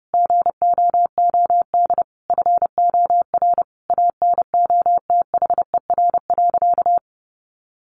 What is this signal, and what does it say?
Morse code, heard as a keyed tone, about 30 wpm, 700 Hz. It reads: GOOD FOR ANOTHER.
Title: Sentences from Top 1400 Words 30wpm